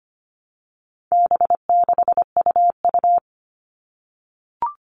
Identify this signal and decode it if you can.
Morse code, heard as a keyed tone, about 25 wpm, 700 Hz. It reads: B6UU
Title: All Cut Numbers - 4-Digit 25wpm